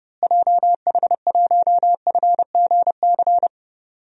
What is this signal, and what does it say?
Morse code, heard as a keyed tone, about 30 wpm, 700 Hz. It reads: JH1FGC
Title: DXCC-100 Call Signs - 6 Characters - Repeated 3x - 30wpm